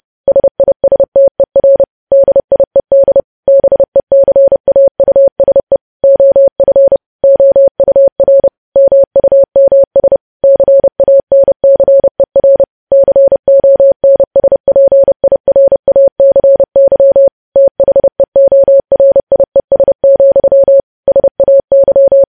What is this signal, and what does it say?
Morse code, heard as a keyed tone, about 30 wpm, 550 Hz. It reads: SISTER DIED BECAUSE OF OUR MUMS CANCER CONSPIRACY THEORIES, SAY
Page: news headlines in morse code at 30 wpm